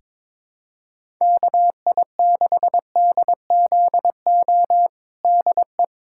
Morse code, heard as a keyed tone, about 22 wpm, 700 Hz. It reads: KI6DZO DE
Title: USA Callsign DE Callsign 22wpm